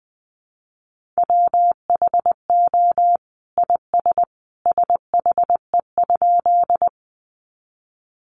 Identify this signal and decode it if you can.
Morse code, heard as a keyed tone, about 20 wpm, 700 Hz. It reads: WHO IS SHE?
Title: Sentences from Top 100 Words 20wpm